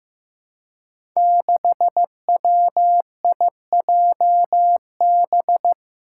Morse code, heard as a keyed tone, about 15 wpm, 700 Hz. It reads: 6WIJB
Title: Five Challenging Letter-Number Combinations 15wpm